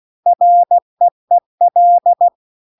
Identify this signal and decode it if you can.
Morse code, heard as a keyed tone, about 16 wpm, 700 Hz. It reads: REEL